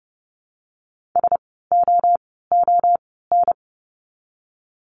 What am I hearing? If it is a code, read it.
Morse code, heard as a keyed tone, about 30 wpm, spaced as for 17 wpm, 700 Hz. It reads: SOON
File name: Lesson 16f - Letter c - Words 30wpm